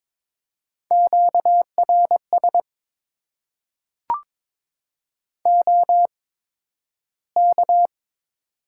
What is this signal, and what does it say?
Morse code, heard as a keyed tone, about 22 wpm, spaced as for 12 wpm, 700 Hz. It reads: QRS OK
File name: Top 90 QSO Elements - 8x Character Spacing - 22wpm